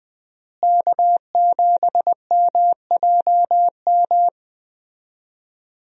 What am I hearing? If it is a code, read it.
Morse code, heard as a keyed tone, about 20 wpm, 700 Hz. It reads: K7MJM
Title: USA Call Signs - 5 Characters - Repeated 3x - 20wpm